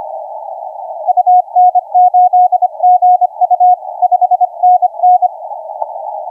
Morse code, heard as a keyed tone, about 25 wpm, 720 Hz, with radio noise. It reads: UN8GU 5NN
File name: 9N7AA 0650UTC 10m